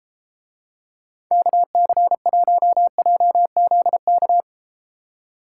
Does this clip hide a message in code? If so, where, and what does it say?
Yes: Morse code from 1.3 to 4.4 s, heard as a keyed tone, about 34 wpm, 700 Hz. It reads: KC1JZK